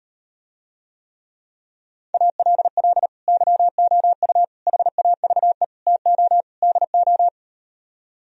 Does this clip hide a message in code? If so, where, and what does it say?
Yes: Morse code from 2.1 to 7.3 s, heard as a keyed tone, about 38 wpm, 700 Hz. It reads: ALL YOU HAVE TO DO